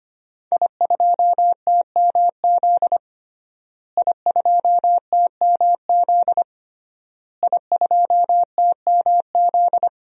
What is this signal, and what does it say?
Morse code, heard as a keyed tone, about 25 wpm, 700 Hz. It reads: I2TMZ I2TMZ I2TMZ
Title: DXCC-100 Call Signs - 5 Characters - Repeated 3x - 25wpm